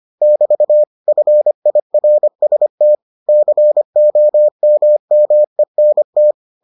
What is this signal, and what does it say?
Morse code, heard as a keyed tone, about 25 wpm, 600 Hz. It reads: <BT> FIRST COMMENT